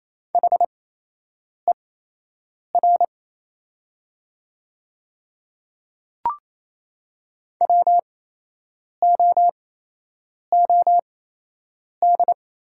Morse code, heard as a keyed tone, about 28 wpm, spaced as for 8 wpm, 700 Hz. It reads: HER WOOD
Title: 8x Character Spacing - Top 300 Words - 28wpm